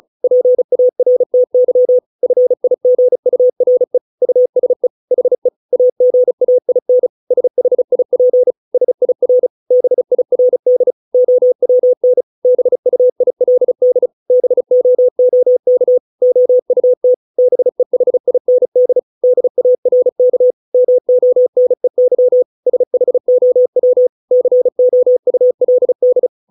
Morse code, heard as a keyed tone, about 34 wpm, 500 Hz. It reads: PARTY FIGURE USE HE AGAIN SHIP SIR BIRD OWN BUILD BOOK OUT BEHIND DARK MONEY SHOW COULD